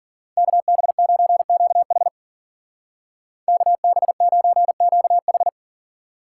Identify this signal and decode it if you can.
Morse code, heard as a keyed tone, about 50 wpm, 700 Hz. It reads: KB9QH KB9QH
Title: Speed Racing - USA 2x2 Call Signs 40wpm